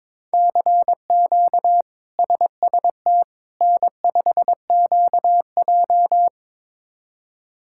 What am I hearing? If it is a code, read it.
Morse code, heard as a keyed tone, about 22 wpm, 700 Hz. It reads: CQ SST N5QJ
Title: K1USN SST QSO Practice - 22wpm